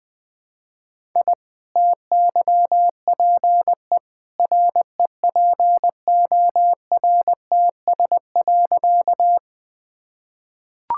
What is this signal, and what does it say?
Morse code, heard as a keyed tone, about 20 wpm, 700 Hz. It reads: I TYPE REPORTS.
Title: Easy Sentences with Hints - 20wpm